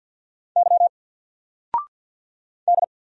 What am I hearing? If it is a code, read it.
Morse code, heard as a keyed tone, about 50 wpm, 700 Hz. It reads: Y D
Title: Single Letter-Number 50wpm